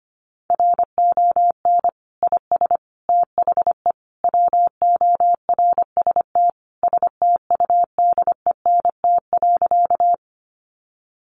Morse code, heard as a keyed tone, about 25 wpm, 700 Hz. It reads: RON IS THE WORST STUDENT.
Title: Additional Easy Phrases 25wpm